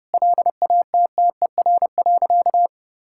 Morse code, heard as a keyed tone, about 30 wpm, 700 Hz. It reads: LATTER.
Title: Sentences from Top 1600 Words 30wpm